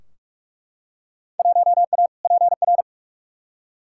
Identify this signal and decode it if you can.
Morse code, heard as a keyed tone, about 43 wpm, 700 Hz. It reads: 1A PR
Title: Field Day Exchanges - 45wpm